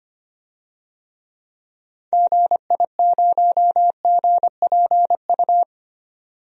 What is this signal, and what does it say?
Morse code, heard as a keyed tone, about 25 wpm, 700 Hz. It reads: GI0GPU
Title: DXCC-100 Call Signs - 6 Characters - 25wpm